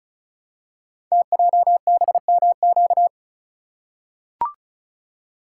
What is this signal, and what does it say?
Morse code, heard as a keyed tone, about 35 wpm, 700 Hz. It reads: TJBMQ
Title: Code Groups - 5 Letters - 35wpm